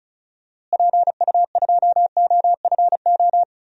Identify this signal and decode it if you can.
Morse code, heard as a keyed tone, about 36 wpm, 700 Hz. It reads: PU2OFO
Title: DXCC-100 Call Signs - 6 Characters - Repeated 3x - 35wpm